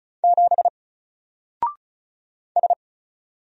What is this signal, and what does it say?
Morse code, heard as a keyed tone, about 36 wpm, 700 Hz. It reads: 7 S
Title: Single Character (Letter:Number:Punctuation:Prosign) 35wpm